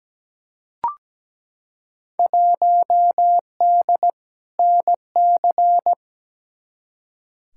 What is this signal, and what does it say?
Morse code, heard as a keyed tone, about 17 wpm, 700 Hz. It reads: 1D NC
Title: Field Day Exchanges - 17wpm